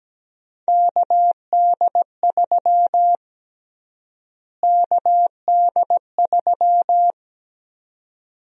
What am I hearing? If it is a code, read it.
Morse code, heard as a keyed tone, about 17 wpm, 700 Hz. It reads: KD3 KD3
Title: USA Call Sign Prefixes - Repeated 2x - 17wpm